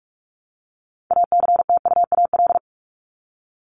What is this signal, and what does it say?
Morse code, heard as a keyed tone, about 44 wpm, 700 Hz. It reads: ACTUAL
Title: Top 751-1000 Words 45wpm